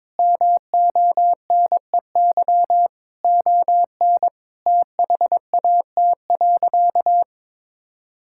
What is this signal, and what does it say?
Morse code, heard as a keyed tone, about 22 wpm, 700 Hz. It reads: MONEY ON THAT.
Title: Sentences from Top 1600 Words 22wpm